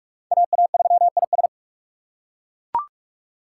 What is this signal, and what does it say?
Morse code, heard as a keyed tone, about 40 wpm, 700 Hz. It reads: AA3IS